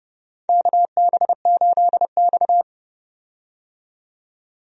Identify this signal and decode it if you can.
Morse code, heard as a keyed tone, about 31 wpm, 700 Hz. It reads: KB8X
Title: USA Call Signs - 4 Characters - 30wpm